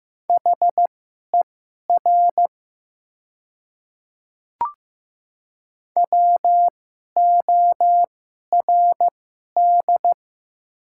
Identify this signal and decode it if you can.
Morse code, heard as a keyed tone, about 15 wpm, spaced as for 11 wpm, 700 Hz. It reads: HER WORD
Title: 2x Character Spacing - Top 300 Words - 15wpm